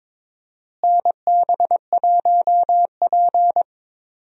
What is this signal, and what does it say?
Morse code, heard as a keyed tone, about 22 wpm, 700 Hz. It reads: NB1P
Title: USA Call Signs - 4 Characters - Repeated 3x - 22wpm